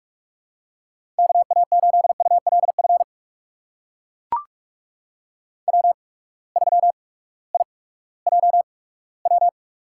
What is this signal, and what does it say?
Morse code, heard as a keyed tone, about 44 wpm, spaced as for 20 wpm, 690 Hz. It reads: KA8ULF W3IJW
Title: USA Call Signs - 8x Character Spacing - 45wpm